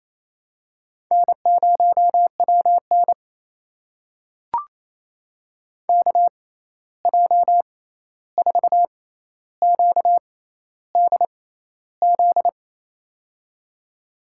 Morse code, heard as a keyed tone, about 28 wpm, spaced as for 13 wpm, 700 Hz. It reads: N0WN KJ4QDZ
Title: USA Call Signs - 6x Character Spacing - 28wpm